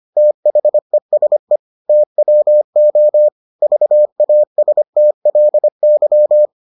Morse code, heard as a keyed tone, about 25 wpm, 600 Hz. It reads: THESE TWO VASTLY